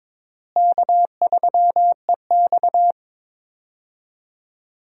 Morse code, heard as a keyed tone, about 22 wpm, 700 Hz. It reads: K3EX
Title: USA Call Signs - 4 Characters - Repeated 3x - 22wpm